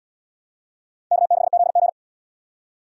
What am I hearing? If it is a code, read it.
Morse code, heard as a keyed tone, about 68 wpm, 700 Hz. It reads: K6ZG